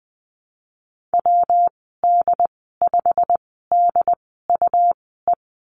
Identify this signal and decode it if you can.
Morse code, heard as a keyed tone, about 20 wpm, spaced as for 15 wpm, 700 Hz. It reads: WD5DUE
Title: 154 Lesson 25g - Letter p - Call Signs 20wpm